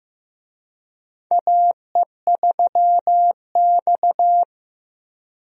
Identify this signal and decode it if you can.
Morse code, heard as a keyed tone, about 16 wpm, 700 Hz. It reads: AE3X
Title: USA Call Signs - 4 Characters - Repeated 2x - 15wpm